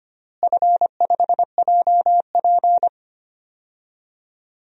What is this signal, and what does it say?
Morse code, heard as a keyed tone, about 25 wpm, 700 Hz. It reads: F5JP
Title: DXCC-100 Call Signs - 4 Characters - Repeated 2x - 25wpm